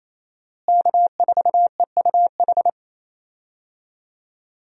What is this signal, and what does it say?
Morse code, heard as a keyed tone, about 28 wpm, 700 Hz. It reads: K4EUH
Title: USA Call Signs - 5 Characters - Repeated 3x - 28wpm